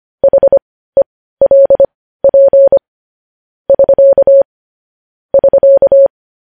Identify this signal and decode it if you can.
Morse code, heard as a keyed tone, about 25 wpm, spaced as for 15 wpm, 550 Hz. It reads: HELP <SK> <SK>
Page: news-headlines-in-morse-code-at-15-wpm